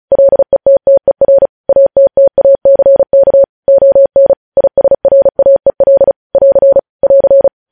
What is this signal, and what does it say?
Morse code, heard as a keyed tone, about 36 wpm, 550 Hz. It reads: LETTER ATTACK ON ISRAEL <AR> <AR>
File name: podcast / News headlines-1x-35x35